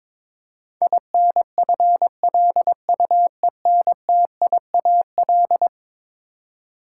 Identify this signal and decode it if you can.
Morse code, heard as a keyed tone, about 22 wpm, 700 Hz. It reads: INFLUENTIAL
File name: practice / 22wpm / Eleven Letter Words - 22wpm